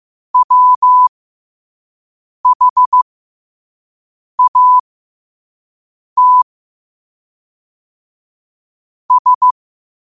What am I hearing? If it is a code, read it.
Morse code, heard as a keyed tone, about 15 wpm, spaced as for 5 wpm, 1000 Hz. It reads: WHAT S
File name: qotdfiles / 5wpm